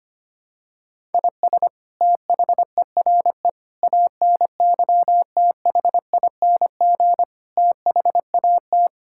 Morse code, heard as a keyed tone, about 25 wpm, 700 Hz. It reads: IS THERE ANYTHING THAT
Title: Sentences from Top 1000 Words - Review Part 6 - 25wpm